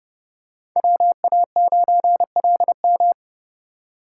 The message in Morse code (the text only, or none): WA9LM